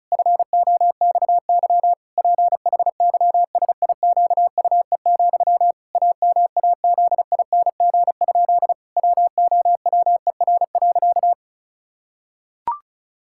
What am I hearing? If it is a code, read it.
Morse code, heard as a keyed tone, about 34 wpm, 700 Hz. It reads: FOXY PHYSIQUE, AMAZING? WOWER.
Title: Pangram Challenge v2 - 35wpm